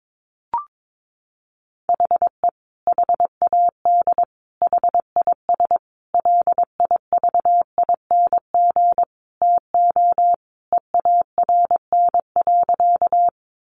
Morse code, heard as a keyed tone, about 22 wpm, 700 Hz. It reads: HE HAD HIS LIVING TO EARN.
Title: Sentences from Top 1500 Words 22wpm